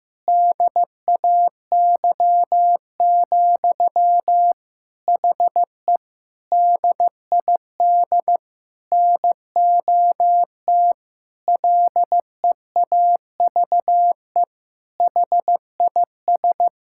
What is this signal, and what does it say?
Morse code, heard as a keyed tone, about 15 wpm, 700 Hz. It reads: DAY, HE DID NOT LEAVE HIS
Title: Sentences from Top 1000 Words - Review Part 6 - 15wpm